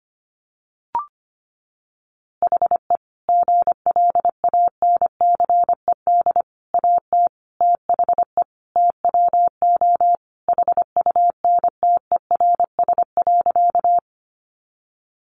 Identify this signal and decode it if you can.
Morse code, heard as a keyed tone, about 25 wpm, 700 Hz. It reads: HE GLANCED AT THE TWO HUNTERS.
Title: A Sound Of Thunder 25wpm